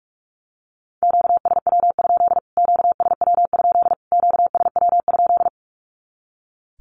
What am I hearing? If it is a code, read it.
Morse code, heard as a keyed tone, about 44 wpm, 700 Hz. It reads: QSW? QSW? QSW?
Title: Q-Codes Repeated 3x 45wpm